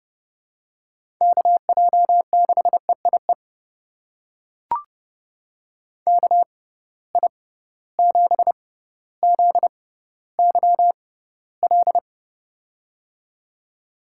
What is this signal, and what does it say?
Morse code, heard as a keyed tone, about 30 wpm, spaced as for 15 wpm, 700 Hz. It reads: KJ6EIE KI7ZYL
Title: USA Call Signs - 6 Characters - 6x Character Spacing - 30wpm